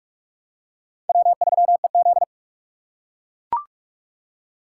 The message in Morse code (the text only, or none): W3EZ